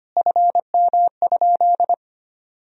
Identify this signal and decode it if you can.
Morse code, heard as a keyed tone, about 25 wpm, 700 Hz. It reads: FM?
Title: Sentences from Top 200 Words and Abbreviations 25wpm